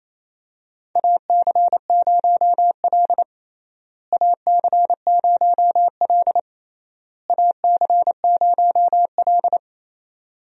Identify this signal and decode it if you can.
Morse code, heard as a keyed tone, about 28 wpm, 700 Hz. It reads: AC0L AC0L AC0L
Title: USA Call Signs - 4 Characters - Repeated 3x - 28wpm